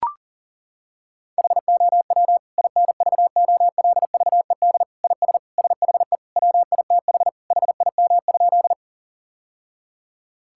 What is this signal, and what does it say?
Morse code, heard as a keyed tone, about 37 wpm, 700 Hz. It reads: HOW INVOLVED IS SHE WITH HIM?
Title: Sentences from Top 1000 Words - Review Part 5 - 40wpm